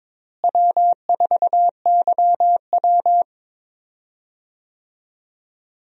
Morse code, heard as a keyed tone, about 22 wpm, 700 Hz. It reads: W4YW
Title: USA Call Signs - 4 Characters - Repeated 2x - 22wpm